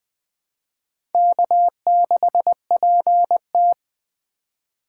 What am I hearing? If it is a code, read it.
Morse code, heard as a keyed tone, about 20 wpm, 700 Hz. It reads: K6PT
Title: USA Call Signs - 2x Character Spacing - 20wpm